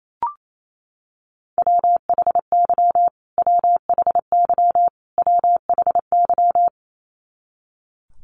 Morse code, heard as a keyed tone, about 28 wpm, 700 Hz. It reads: WHY WHY WHY
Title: Top 300 Words Repeated 28wpm